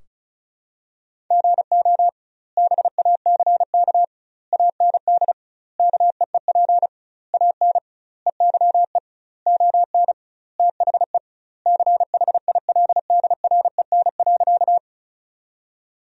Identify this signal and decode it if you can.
Morse code, heard as a keyed tone, about 34 wpm, 700 Hz. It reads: GO BACK AND KEEP AN EYE ON THE CHILDREN.
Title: 2x Word-Spacing - Sentences from Top 300 Words - 35wpm